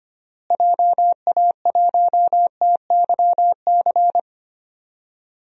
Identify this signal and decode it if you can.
Morse code, heard as a keyed tone, about 25 wpm, 700 Hz. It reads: JA1TYC
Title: DXCC-100 Call Signs - 6 Characters - Repeated 2x - 25wpm